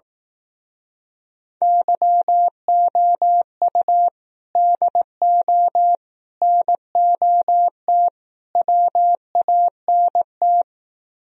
Morse code, heard as a keyed tone, about 18 wpm, 700 Hz. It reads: YOU DO NOT WANT
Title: Speed-Racing - Sets of 4 Words 15wpm